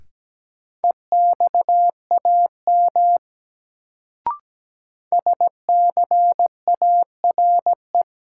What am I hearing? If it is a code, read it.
Morse code, heard as a keyed tone, about 17 wpm, 700 Hz. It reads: EXAM SCARE
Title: Sentences from Top 1000 Words - Review Part 5 - New Word Drill - 17wpm